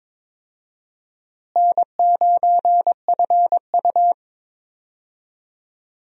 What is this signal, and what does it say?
Morse code, heard as a keyed tone, about 22 wpm, 700 Hz. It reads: N9FU
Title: USA Call Signs - 4 Characters - Repeated 3x - 22wpm